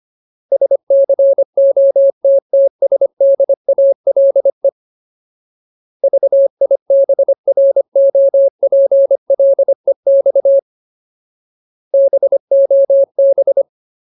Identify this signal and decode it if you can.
Morse code, heard as a keyed tone, about 25 wpm, 550 Hz. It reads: SCOTTSDALE VIBROPLEX BOB
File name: Ultimate Word List / 25WPM Ultimate